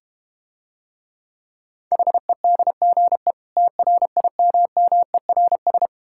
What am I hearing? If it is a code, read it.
Morse code, heard as a keyed tone, about 31 wpm, 700 Hz. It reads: HEDGE TRIMMERS